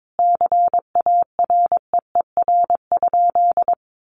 Morse code, heard as a keyed tone, about 22 wpm, 700 Hz. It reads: CAREER?
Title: Sentences from Top 1000 Words 22wpm